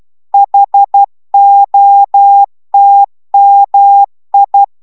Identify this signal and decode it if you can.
Morse code, heard as a keyed tone, about 12 wpm, 800 Hz. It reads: HOTMI